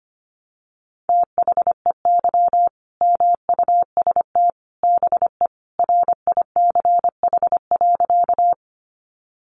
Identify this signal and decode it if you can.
Morse code, heard as a keyed tone, about 25 wpm, 700 Hz. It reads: THEY MUST BE RICH.
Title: Sentences from Top 1100 Words 25wpm